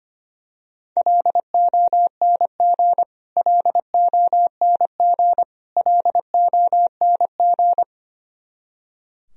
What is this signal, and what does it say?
Morse code, heard as a keyed tone, about 25 wpm, 700 Hz. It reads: LONG LONG LONG
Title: Top 100 Words Repeated 25wpm